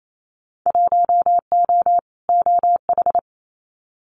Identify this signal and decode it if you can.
Morse code, heard as a keyed tone, about 28 wpm, 700 Hz. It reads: 1O OH
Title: Winter Field Day Exchanges 28wpm